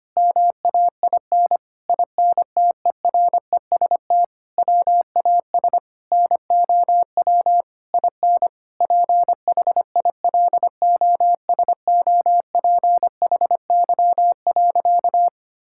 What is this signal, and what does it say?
Morse code, heard as a keyed tone, about 25 wpm, 700 Hz. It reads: MAIN INTEREST WAS NOW IN PHILOSOPHY.
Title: Sentences from Top 1000 Words - Review Part 6 - 25wpm